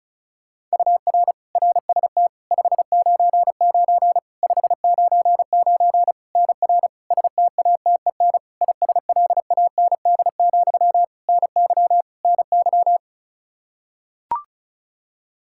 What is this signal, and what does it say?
Morse code, heard as a keyed tone, about 34 wpm, 700 Hz. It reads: UR RST 599 599 NR STATEN ISLAND, NY NY
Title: Practice QSO - Exchange 1 - Beginning - 35wpm